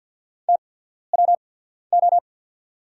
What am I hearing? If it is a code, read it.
Morse code, heard as a keyed tone, about 50 wpm, spaced as for 14 wpm, 690 Hz. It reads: TWO
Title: 8x Character Spacing - Top 100 Words - 50wpm